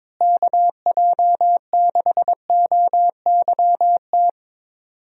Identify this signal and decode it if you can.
Morse code, heard as a keyed tone, about 22 wpm, 700 Hz. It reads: KJ6OYT
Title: USA Call Signs - 6 Characters - Repeated 2x - 22wpm